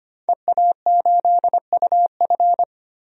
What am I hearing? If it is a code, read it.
Morse code, heard as a keyed tone, about 25 wpm, 700 Hz. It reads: EA8UF
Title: DXCC-100 Call Signs - 5 Characters - 25wpm